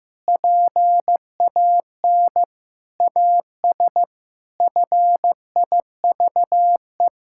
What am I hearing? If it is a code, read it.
Morse code, heard as a keyed tone, about 15 wpm, 700 Hz. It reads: PAN AS FIVE
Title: Pangram Challenge v2 - 15wpm